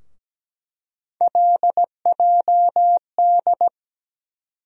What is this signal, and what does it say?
Morse code, heard as a keyed tone, about 17 wpm, 700 Hz. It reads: LJD
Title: USA Call Sign Postfixes - 3 Letters 17wpm